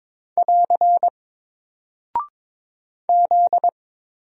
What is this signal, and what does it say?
Morse code, heard as a keyed tone, about 22 wpm, 700 Hz. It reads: <AR> Z
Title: Single Character (Letter:Number:Punctuation:Prosign) Alt 22wpm